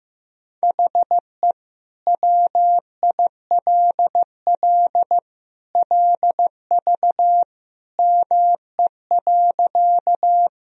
Morse code, heard as a keyed tone, about 15 wpm, 700 Hz. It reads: HE WILL LV ME.
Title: Sentences from Top 400 Words and Abbreviations 15wpm